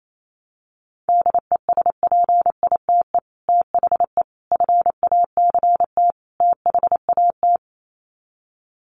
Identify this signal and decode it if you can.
Morse code, heard as a keyed tone, about 28 wpm, 700 Hz. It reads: DESPITE THE FACT THAT
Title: Sets of 4 Words 28wpm